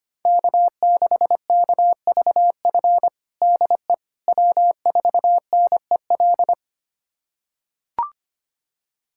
Morse code, heard as a keyed tone, about 25 wpm, 700 Hz. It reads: K6KVF DE W4NEL
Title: USA Callsign DE Callsign - 5 Characters - 25wpm